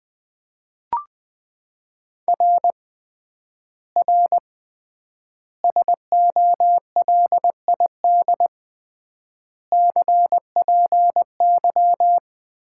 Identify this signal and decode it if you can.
Morse code, heard as a keyed tone, about 20 wpm, 700 Hz. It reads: R R SOLID CPY
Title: Common QSO Phrases - 3x Word-Spacing - 20wpm